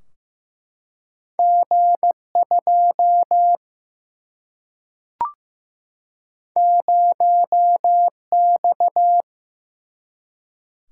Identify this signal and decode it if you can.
Morse code, heard as a keyed tone, about 15 wpm, 700 Hz. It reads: G2 0X
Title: Single-Letter Single-Number 15wpm